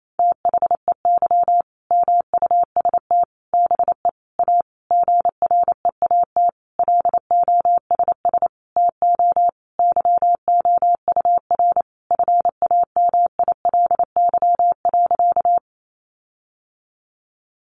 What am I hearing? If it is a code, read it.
Morse code, heard as a keyed tone, about 28 wpm, 700 Hz. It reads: THEY MUST BE A GREAT LOSS TO YOUR FAMILY.
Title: Sentences from Top 1400 Words 28wpm